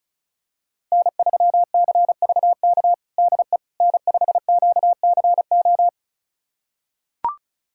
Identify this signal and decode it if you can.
Morse code, heard as a keyed tone, about 36 wpm, 700 Hz. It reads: N3CVK DE N5QCO
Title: USA Callsign DE Callsign - 5 Characters - 35wpm